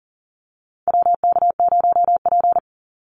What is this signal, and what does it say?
Morse code, heard as a keyed tone, about 40 wpm, 700 Hz. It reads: WK0P
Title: Call Signs - 40wpm